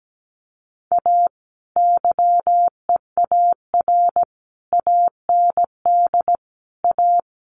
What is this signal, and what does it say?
Morse code, heard as a keyed tone, about 17 wpm, 700 Hz. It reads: A YEAR AND A
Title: Sets of 5 Words 17wpm